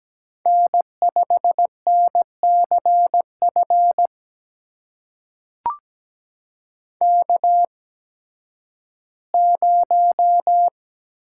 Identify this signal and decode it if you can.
Morse code, heard as a keyed tone, about 17 wpm, spaced as for 11 wpm, 700 Hz. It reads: N5NCF K0
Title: USA Call Signs - 8x Character Spacing - 17wpm